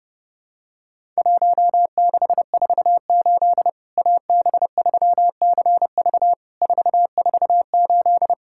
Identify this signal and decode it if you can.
Morse code, heard as a keyed tone, about 30 wpm, 700 Hz. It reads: 1648 AB3CV 448
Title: Worked All Europe DX Contest - QTC Exchange - 30wpm